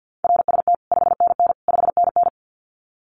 Morse code, heard as a keyed tone, about 50 wpm, 700 Hz. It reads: RST 5NN 5NN
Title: Common QSO Phrases Encore 50wpm